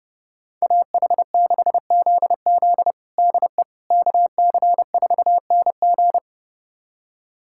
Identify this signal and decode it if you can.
Morse code, heard as a keyed tone, about 32 wpm, 700 Hz. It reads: AH6ZZ DE KC4NG
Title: USA Callsign DE Callsign - 5 Characters - 30wpm